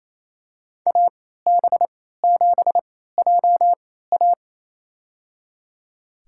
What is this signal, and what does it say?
Morse code, heard as a keyed tone, about 27 wpm, spaced as for 16 wpm, 700 Hz. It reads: AB7JA
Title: USA Call Signs - 3x Character Spacing - 28wpm